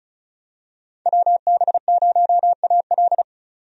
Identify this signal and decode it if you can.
Morse code, heard as a keyed tone, about 36 wpm, 700 Hz. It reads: WB0AL
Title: USA Call Signs - 5 Characters - 35wpm